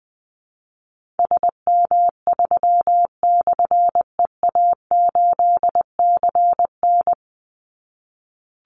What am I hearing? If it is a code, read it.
Morse code, heard as a keyed tone, about 20 wpm, 700 Hz. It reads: SM3/EA8CN
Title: Call Signs - 20wpm